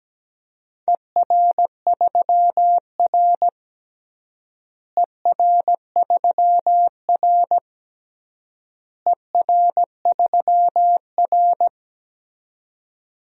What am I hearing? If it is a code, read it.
Morse code, heard as a keyed tone, about 17 wpm, 700 Hz. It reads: ER3R ER3R ER3R
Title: DXCC-100 Call Signs - 4 Characters - Repeated 3x - 17wpm